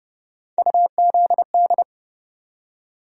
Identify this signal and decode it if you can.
Morse code, heard as a keyed tone, about 30 wpm, 700 Hz. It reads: UZD
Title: USA Call Sign Postfixes - 3 Letters 30wpm